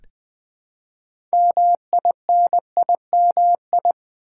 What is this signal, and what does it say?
Morse code, heard as a keyed tone, about 20 wpm, 700 Hz. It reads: MINIMI
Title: Build A Word - 3 Letters At A Time - 8-Letter Words - 20wpm